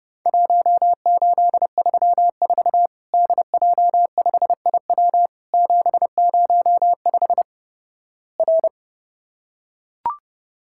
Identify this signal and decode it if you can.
Morse code, heard as a keyed tone, about 31 wpm, 700 Hz. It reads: 1834 DJ5IW 705 R
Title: Worked All Europe DX Contest - QTC Exchange - 30wpm